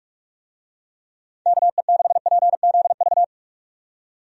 Morse code, heard as a keyed tone, about 46 wpm, 700 Hz. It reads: KE6PZV
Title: USA Call Signs - 6 Characters - 45wpm